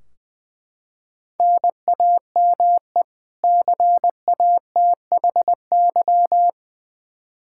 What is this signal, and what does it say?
Morse code, heard as a keyed tone, about 20 wpm, 700 Hz. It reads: NAME CATHY
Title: Practice QSO - Exchange 1 - Beginning - 20wpm